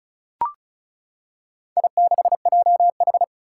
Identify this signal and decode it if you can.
Morse code, heard as a keyed tone, about 36 wpm, 700 Hz. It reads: I6JH
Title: Four Challenging Letter-Number Combinations 35wpm